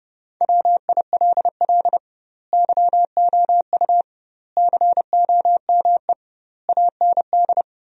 Morse code, heard as a keyed tone, about 30 wpm, 700 Hz. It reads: WILL YOU COME AND
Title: 2x Word-Spacing - Sentences from Top 100 Words - 30wpm